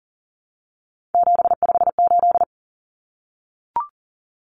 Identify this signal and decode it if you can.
Morse code, heard as a keyed tone, about 40 wpm, 700 Hz. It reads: 758
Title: Numbers - 3-Digit 40wpm